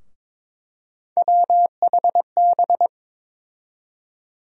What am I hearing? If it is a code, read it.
Morse code, heard as a keyed tone, about 22 wpm, 700 Hz. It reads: WHB
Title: USA Call Sign Postfixes - 3 Letters 22wpm